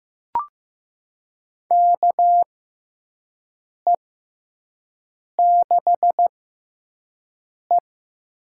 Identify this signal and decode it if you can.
Morse code, heard as a keyed tone, about 15 wpm, spaced as for 5 wpm, 700 Hz. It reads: KE6E
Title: USA Call Signs - 4 Characters - 6x Character Spacing - 15wpm